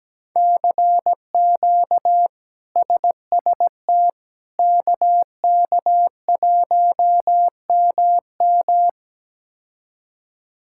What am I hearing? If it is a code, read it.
Morse code, heard as a keyed tone, about 17 wpm, 700 Hz. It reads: CQ SST KK1MM
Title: K1USN SST QSO Practice - 17wpm